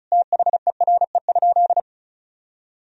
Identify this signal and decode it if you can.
Morse code, heard as a keyed tone, about 36 wpm, 700 Hz. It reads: THERE?
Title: Sentences from Top 100 Words and Abbreviations 35wpm